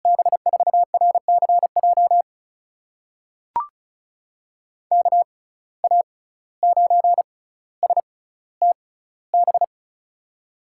Morse code, heard as a keyed tone, about 36 wpm, spaced as for 17 wpm, 700 Hz. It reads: B4RCJ KA9STB
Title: USA Call Signs - 6 Characters - 6x Character Spacing - 35wpm